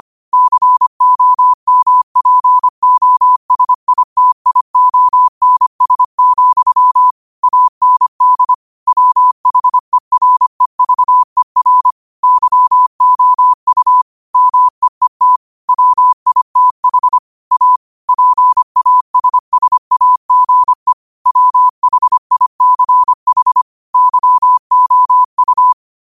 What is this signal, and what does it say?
Morse code, heard as a keyed tone, about 25 wpm, 1000 Hz. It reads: COMPOSITIONS, AND WHEREVER YOU MEET WITH A PASSAGE WHICH YOU